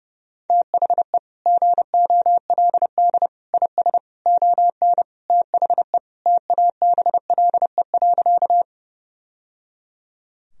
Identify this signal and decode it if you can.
Morse code, heard as a keyed tone, about 32 wpm, 700 Hz. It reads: THE GOLD IS ON THE TABLE.
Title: Sentences from Top 1000 Words - Review Part 3 - 30wpm